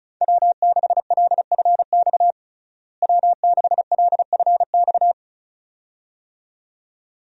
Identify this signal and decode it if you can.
Morse code, heard as a keyed tone, about 36 wpm, 700 Hz. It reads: W6LFX W6LFX
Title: USA Call Signs - 5 Characters - Repeated 2x - 35wpm